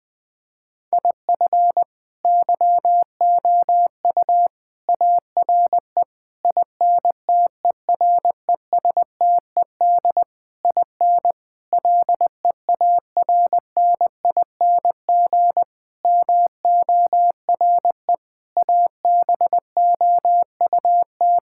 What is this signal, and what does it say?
Morse code, heard as a keyed tone, about 20 wpm, 700 Hz. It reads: IF YOU ARE INTERESTED IN LEARNING MORE ABOUT
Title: Sentences with 8-Word Phrases - 20wpm